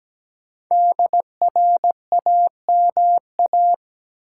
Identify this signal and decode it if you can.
Morse code, heard as a keyed tone, about 17 wpm, 700 Hz. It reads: DRAMA
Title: Five Letter Words - 17wpm